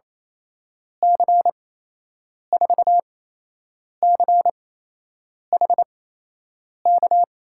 Morse code, heard as a keyed tone, about 28 wpm, spaced as for 8 wpm, 700 Hz. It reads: C4CHK
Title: USA Call Signs - 6 Characters - 8x Character Spacing - 28wpm